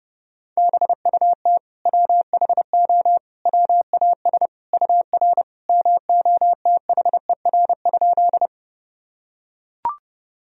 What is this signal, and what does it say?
Morse code, heard as a keyed tone, about 30 wpm, 700 Hz. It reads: BUT WHO WAS UR MOTHER?
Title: Sentences from Top 200 Words and Abbreviations 30wpm